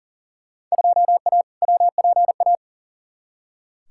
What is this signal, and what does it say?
Morse code, heard as a keyed tone, about 40 wpm, 700 Hz. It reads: 2A WPA
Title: Field Day Exchanges - 40wpm